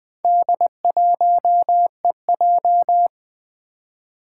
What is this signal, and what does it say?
Morse code, heard as a keyed tone, about 20 wpm, 700 Hz. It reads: D1EJ
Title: Four Challenging Letter-Number Combinations 20wpm